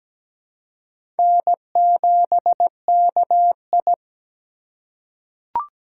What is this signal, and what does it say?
Morse code, heard as a keyed tone, about 17 wpm, 700 Hz. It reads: N7KI
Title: USA Call Signs - 4 Characters - 4x Character Spacing - 17wpm